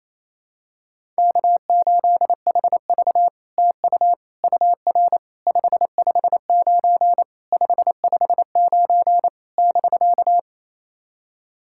Morse code, heard as a keyed tone, about 28 wpm, 700 Hz. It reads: K8HV TU UR 559 559 <BK>
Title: SOTA QSO Practice 28wpm